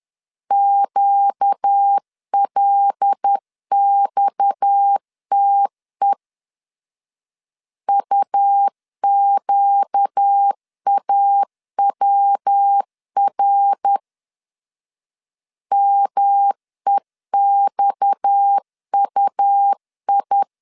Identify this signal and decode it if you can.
Morse code, heard as a keyed tone, about 10 wpm, 800 Hz. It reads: QLXTE UQAWR MEXUI
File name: LEZIONI / 6 PKX0 / 6 PKX 10wpm 2426